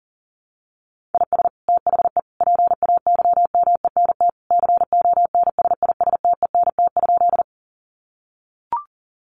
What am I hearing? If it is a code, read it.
Morse code, heard as a keyed tone, about 40 wpm, 700 Hz. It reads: IS THE PAYMENT CONSISTENT?
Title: Sentences from Top 1500 Words 40wpm